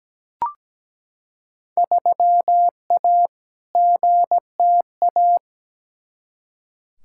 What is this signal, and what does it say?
Morse code, heard as a keyed tone, about 17 wpm, 700 Hz. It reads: 3A GTA
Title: Field Day Exchanges - 17wpm